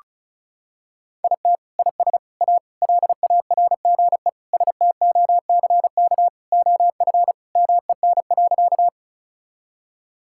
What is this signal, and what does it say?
Morse code, heard as a keyed tone, about 36 wpm, 700 Hz. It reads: IT IS A LARGE STOCK OF MEN.
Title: Sentences from Top 1000 Words - Review Part 5 - 35wpm